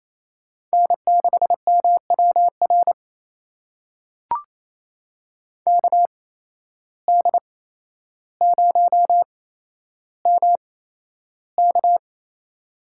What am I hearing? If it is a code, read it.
Morse code, heard as a keyed tone, about 28 wpm, spaced as for 12 wpm, 700 Hz. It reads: N6MWR KD0MK
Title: USA Call Signs - 5 Characters - 8x Character Spacing - 28wpm